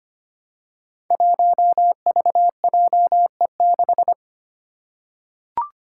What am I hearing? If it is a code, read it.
Morse code, heard as a keyed tone, about 25 wpm, 700 Hz. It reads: 1VJE6
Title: Five Challenging Letter-Number Combinations 25wpm